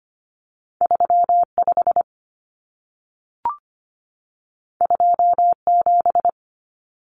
Morse code, heard as a keyed tone, about 25 wpm, 700 Hz. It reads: 35 27
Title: Common Cut Numbers - 2-Digit 25wpm